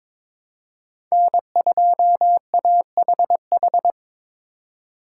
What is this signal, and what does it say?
Morse code, heard as a keyed tone, about 22 wpm, 700 Hz. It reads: N2AHH
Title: USA Call Signs - 5 Characters - 4x Character Spacing - 22wpm